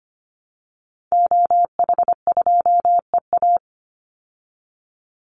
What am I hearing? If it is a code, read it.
Morse code, heard as a keyed tone, about 25 wpm, 700 Hz. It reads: OH2EA
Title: Call Signs Repeated 2x - 25wpm